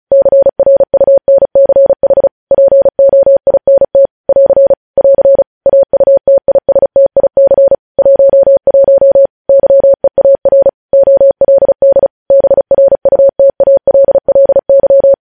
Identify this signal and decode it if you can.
Morse code, heard as a keyed tone, about 36 wpm, 550 Hz. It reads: CRUNCH POINT <AR> <AR> AUTISTIC 11 YEAR OLD BRUTALLY
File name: podcast / News headlines-1x-35x35